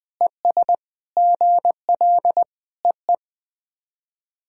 Morse code, heard as a keyed tone, about 20 wpm, 700 Hz. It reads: ES GL EE
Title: SOTA QSO Practice 20wpm